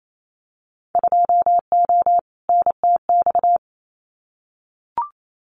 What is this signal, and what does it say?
Morse code, heard as a keyed tone, about 28 wpm, 700 Hz. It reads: 2O NTX
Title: Winter Field Day Exchanges 28wpm